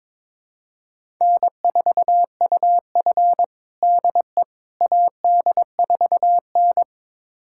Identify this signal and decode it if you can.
Morse code, heard as a keyed tone, about 22 wpm, 700 Hz. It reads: N4UF DE AD4N
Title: USA Callsign DE Callsign - 4 Characters - 22wpm